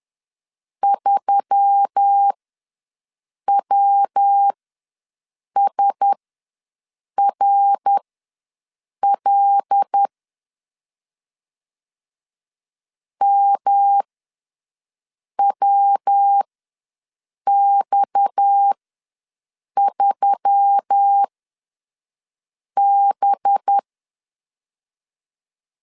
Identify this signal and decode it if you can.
Morse code, heard as a keyed tone, about 10 wpm, spaced as for 5 wpm, 800 Hz. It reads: 3WSRL MWX3B
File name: LEZIONI / 7 GF9Z / 7 G ws 10wpm 3843